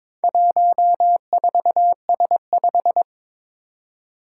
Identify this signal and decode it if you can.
Morse code, heard as a keyed tone, about 22 wpm, 700 Hz. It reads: 14S5
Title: Four Challenging Letter-Number Combinations 22wpm